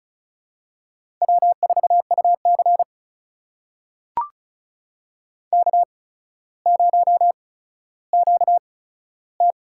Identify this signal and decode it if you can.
Morse code, heard as a keyed tone, about 36 wpm, spaced as for 15 wpm, 700 Hz. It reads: W4UC K0QT
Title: USA Call Signs - 4 Characters - 8x Character Spacing - 35wpm